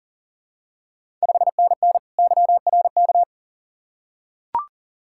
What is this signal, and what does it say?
Morse code, heard as a keyed tone, about 40 wpm, 690 Hz. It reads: 5NN YRK
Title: Ontario QSO Party Exchange 40wpm